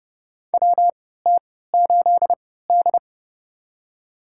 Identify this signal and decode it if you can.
Morse code, heard as a keyed tone, about 30 wpm, spaced as for 17 wpm, 700 Hz. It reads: WT8D
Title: USA Call Signs - 3x Character Spacing - 30wpm